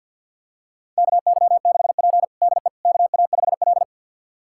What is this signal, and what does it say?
Morse code, heard as a keyed tone, about 50 wpm, 700 Hz. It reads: KY6P DE KA5L
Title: USA Callsign DE Callsign - 4 Characters - 50wpm